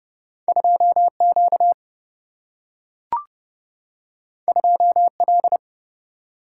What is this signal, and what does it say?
Morse code, heard as a keyed tone, about 30 wpm, 700 Hz. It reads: 2 Q 2 L
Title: Single-Letter Single-Number 30wpm